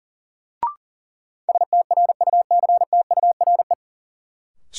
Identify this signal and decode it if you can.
Morse code, heard as a keyed tone, about 39 wpm, 700 Hz. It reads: STRUCTURE
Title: Sentences from Top 1000 Words - Review Part 6 - New Word Drill - 40wpm